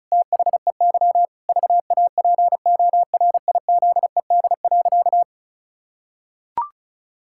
Morse code, heard as a keyed tone, about 36 wpm, 700 Hz. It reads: THEY VAPORIZED.
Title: Pangram Challenge v2 - 35wpm